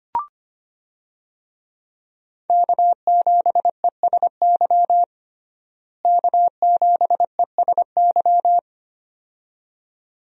Morse code, heard as a keyed tone, about 25 wpm, 700 Hz. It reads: K7ESY K7ESY
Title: USA Call Signs - 5 Characters - Repeated 2x - 25wpm